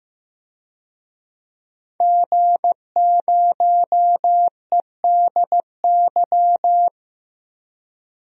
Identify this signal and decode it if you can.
Morse code, heard as a keyed tone, about 15 wpm, 700 Hz. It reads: G0EDY
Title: DXCC-100 Call Signs - 5 Characters - 15wpm